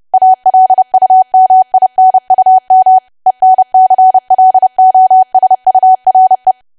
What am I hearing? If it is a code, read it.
Morse code, heard as a keyed tone, about 30 wpm, 750 Hz. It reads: ALUMINUM ENCLOSURE
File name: cwa / W1AW-Level3 / 30WPM